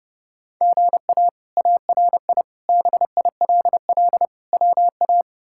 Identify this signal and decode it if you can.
Morse code, heard as a keyed tone, about 32 wpm, 700 Hz. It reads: GA ARI BILL WA
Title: K1USN SST QSO Practice - 30wpm